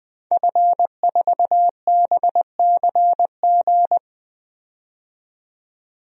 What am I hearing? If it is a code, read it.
Morse code, heard as a keyed tone, about 20 wpm, 700 Hz. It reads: F4BCG
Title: DXCC-100 Call Signs - 5 Characters - 20wpm